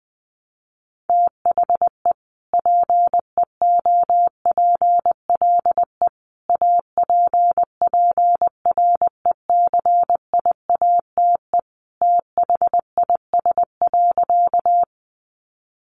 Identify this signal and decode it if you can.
Morse code, heard as a keyed tone, about 20 wpm, 700 Hz. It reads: THE PEOPLE APPRECIATE THIS.
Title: Sentences from Top 1300 Words 20wpm